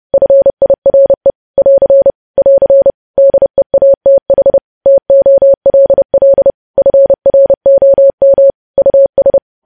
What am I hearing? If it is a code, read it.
Morse code, heard as a keyed tone, about 30 wpm, 550 Hz. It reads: FIRE <AR> <AR> DEATH TOLL FROM US
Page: News Headlines in Morse Code at 30 WPM